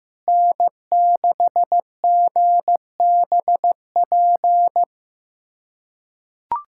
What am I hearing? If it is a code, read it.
Morse code, heard as a keyed tone, about 15 wpm, 700 Hz. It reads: N6GBP
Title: USA Callsign DE Callsign - 5 Characters - 15wpm